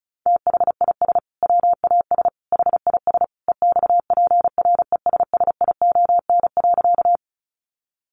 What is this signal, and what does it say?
Morse code, heard as a keyed tone, about 34 wpm, 700 Hz. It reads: THIS WAS HIS EXPRESSION.
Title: Sentences from Top 1200 Words 35wpm